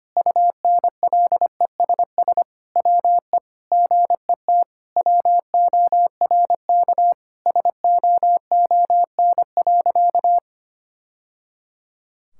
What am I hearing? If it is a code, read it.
Morse code, heard as a keyed tone, about 25 wpm, 700 Hz. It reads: UNLESS WE GET WORK SOON.
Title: Sentences from Top 1000 Words - Review Part 6 - 25wpm